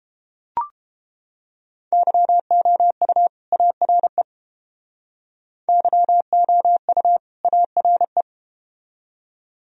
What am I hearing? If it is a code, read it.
Morse code, heard as a keyed tone, about 31 wpm, 700 Hz. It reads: YOU ARE YOU ARE